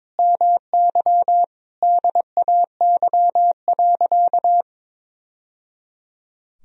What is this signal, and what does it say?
Morse code, heard as a keyed tone, about 22 wpm, 700 Hz. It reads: MY DAY.
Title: Sentences from Top 1000 Words - Review Part 2 - 22wpm